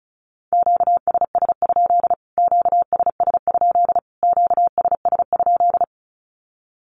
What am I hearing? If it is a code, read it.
Morse code, heard as a keyed tone, about 34 wpm, 700 Hz. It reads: QSS? QSS? QSS?
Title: Q-Codes Repeated 3x 35wpm